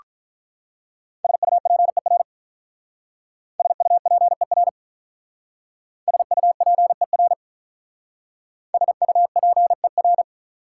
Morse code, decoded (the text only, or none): SUPER SUPER SUPER SUPER